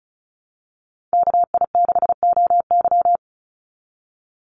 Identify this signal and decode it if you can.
Morse code, heard as a keyed tone, about 34 wpm, 700 Hz. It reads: KI6OY
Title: Call Signs Repeated 3x - 35wpm